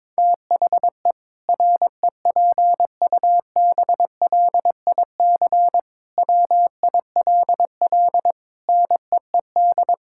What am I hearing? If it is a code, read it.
Morse code, heard as a keyed tone, about 22 wpm, 700 Hz. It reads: THE REPUBLIC WILL NEED
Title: Sentences from Top 1600 Words 22wpm